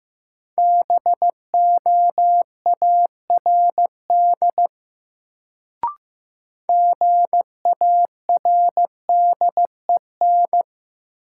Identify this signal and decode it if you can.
Morse code, heard as a keyed tone, about 15 wpm, 700 Hz. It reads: BOARD GARDEN
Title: Sentences from Top 700 Words - New Word Drill - 15wpm